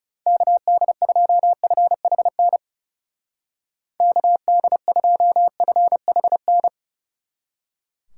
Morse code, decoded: KD2FHN KD2FHN

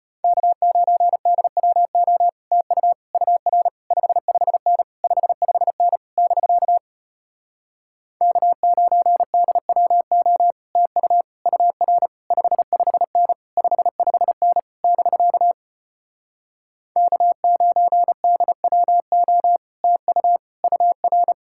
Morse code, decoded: K9DWO TU UR 55N 55N <BK> K9DWO TU UR 55N 55N <BK> K9DWO TU UR

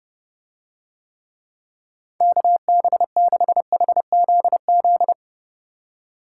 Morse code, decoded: KB6HZZ

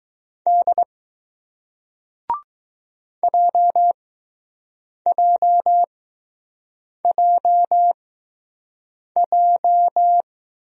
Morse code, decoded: D JJJJ

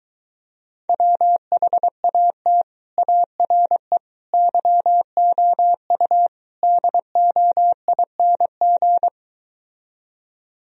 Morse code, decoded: WHAT ARE YOU DOING

